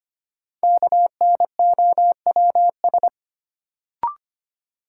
Morse code, heard as a keyed tone, about 25 wpm, 700 Hz. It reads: KNOWS
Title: Sentences from Top 1000 Words - Review Part 1 - New Word Drill - 25wpm